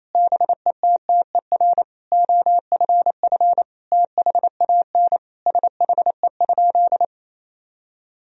Morse code, heard as a keyed tone, about 27 wpm, 700 Hz. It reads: BETTER OFF THAN SHE?
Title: Sentences from Top 1000 Words - Review Part 3 - 28wpm